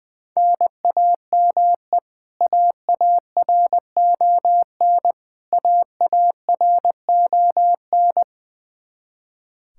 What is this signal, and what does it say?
Morse code, heard as a keyed tone, about 20 wpm, 700 Hz. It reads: NAME AARON AARON
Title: Practice QSO - Exchange 1 - Beginning - 20wpm